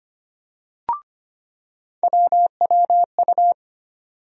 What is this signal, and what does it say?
Morse code, heard as a keyed tone, about 25 wpm, 700 Hz. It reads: WWU
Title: USA Call Sign Postfixes - 3 Letters 25wpm